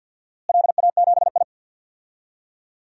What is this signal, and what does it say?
Morse code, heard as a keyed tone, about 51 wpm, 700 Hz. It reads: LA7I
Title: DXCC-100 Call Signs - 4 Characters - Repeated 3x - 50wpm